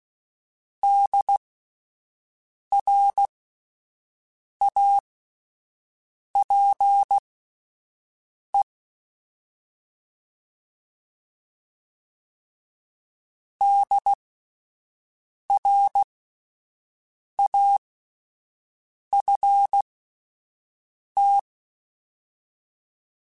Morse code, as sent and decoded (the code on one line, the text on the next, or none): -.. .-. .- .--. / -.. .-. .- ..-. -
DRAPE DRAFT